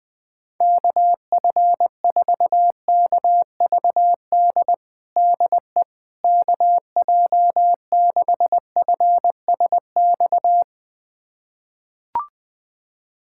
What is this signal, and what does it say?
Morse code, heard as a keyed tone, about 20 wpm, 700 Hz. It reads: KF4KVD DE KJ6FSX
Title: USA Callsign DE Callsign - 6 Characters - 20wpm